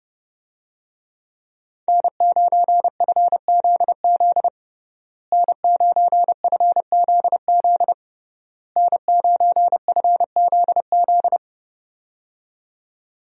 Morse code, heard as a keyed tone, about 30 wpm, 700 Hz. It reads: N9FZZ N9FZZ N9FZZ
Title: USA Call Signs - 5 Characters - Repeated 3x - 30wpm